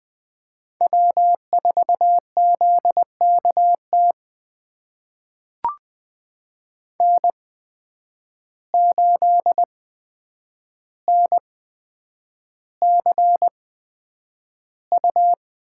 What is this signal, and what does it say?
Morse code, heard as a keyed tone, about 20 wpm, spaced as for 9 wpm, 700 Hz. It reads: W4ZKT N8NCU
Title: USA Call Signs - 5 Characters - 8x Character Spacing - 20wpm